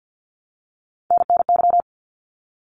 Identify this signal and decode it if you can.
Morse code, heard as a keyed tone, about 50 wpm, 700 Hz. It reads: NNY
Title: ARRL Sections 50wpm